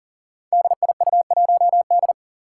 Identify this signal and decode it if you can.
Morse code, heard as a keyed tone, about 40 wpm, 700 Hz. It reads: DIU1D
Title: Five Challenging Letter-Number Combinations 40wpm